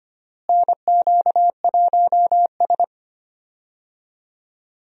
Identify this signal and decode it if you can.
Morse code, heard as a keyed tone, about 25 wpm, 700 Hz. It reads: NQ1S
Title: USA Call Signs - 4 Characters - Repeated 2x - 25wpm